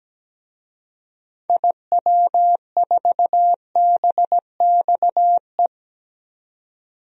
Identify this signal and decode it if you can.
Morse code, heard as a keyed tone, about 17 wpm, 700 Hz. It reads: IW4BXE
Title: DXCC-100 Call Signs - 6 Characters - 17wpm